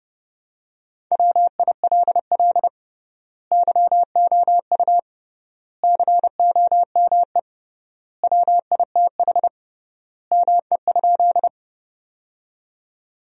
Morse code, heard as a keyed tone, about 32 wpm, 700 Hz. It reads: WILL YOU COME WITH ME?
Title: 3x Word-Spacing - Sentences from Top 200 Words - 30wpm